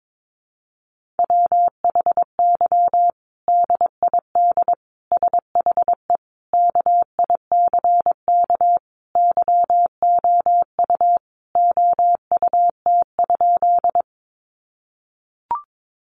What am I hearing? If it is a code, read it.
Morse code, heard as a keyed tone, about 22 wpm, 700 Hz. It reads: WHY DID SHE KICK YOU OUT?
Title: Sentences from Top 1300 Words 22wpm